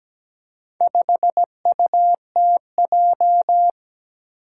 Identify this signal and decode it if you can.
Morse code, heard as a keyed tone, about 17 wpm, 700 Hz. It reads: 5UTJ
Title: Four Challenging Letter-Number Combinations 17wpm